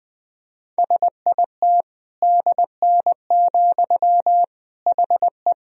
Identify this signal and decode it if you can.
Morse code, heard as a keyed tone, about 20 wpm, 700 Hz. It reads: SIT DN, HE
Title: Sentences from Top 500 Words and Abbreviations 20wpm